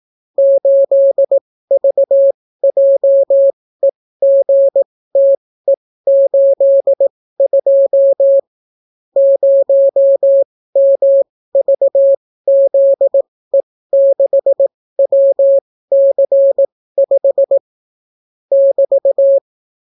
Morse code, heard as a keyed tone, about 18 wpm, spaced as for 14 wpm, 550 Hz. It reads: 8VJEGTE82 0MVZE6WC5 <BT>